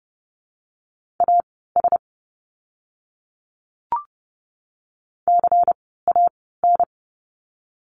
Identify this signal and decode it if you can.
Morse code, heard as a keyed tone, about 30 wpm, spaced as for 17 wpm, 700 Hz. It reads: AS CAN
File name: Lesson 17f - Letter u - Words 30wpm